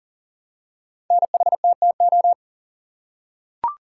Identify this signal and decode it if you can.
Morse code, heard as a keyed tone, about 42 wpm, 690 Hz. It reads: NHTTO
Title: Code Groups - 5 Letters - 40wpm